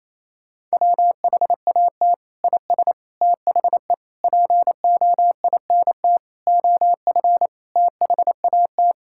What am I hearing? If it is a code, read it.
Morse code, heard as a keyed tone, about 28 wpm, 700 Hz. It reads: WHAT IS THE POINT OF THAT
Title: Sentences from Top 1000 Words - Review Part 5 - 28wpm